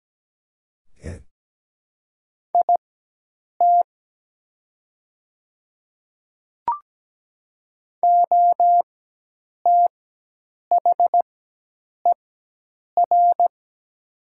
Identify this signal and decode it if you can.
Morse code, heard as a keyed tone, about 17 wpm, spaced as for 8 wpm, 700 Hz. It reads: IT OTHER
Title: 4x Character Spacing - Top 300 Words - 17wpm